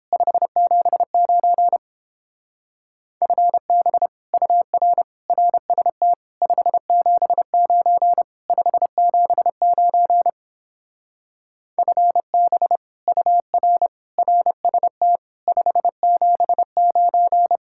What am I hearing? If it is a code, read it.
Morse code, heard as a keyed tone, about 29 wpm, 700 Hz. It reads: 579 FB UR RST 579 579 FB UR RST 579